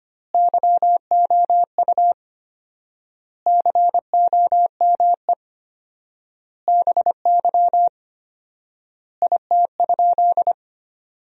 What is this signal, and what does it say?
Morse code, heard as a keyed tone, about 25 wpm, 700 Hz. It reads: YOU COME BY IT?
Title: 4x Word-Spacing - Sentences from Top 100 Words - 25wpm